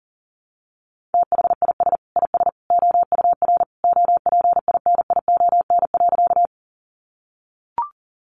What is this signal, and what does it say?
Morse code, heard as a keyed tone, about 40 wpm, 700 Hz. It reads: THIS IS OUR OPINION.
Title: Sentences from Top 1200 Words 40wpm